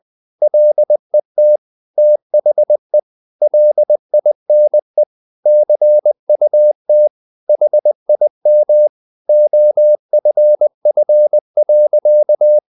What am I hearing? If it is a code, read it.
Morse code, heard as a keyed tone, about 20 wpm, 600 Hz. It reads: LET THE LINE CUT HIM OFF.
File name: the old man and the sea part 2 20wpm